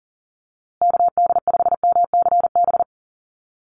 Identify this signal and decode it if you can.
Morse code, heard as a keyed tone, about 40 wpm, 700 Hz. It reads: KD5MCB
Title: USA Call Signs Repeated 3x 40wpm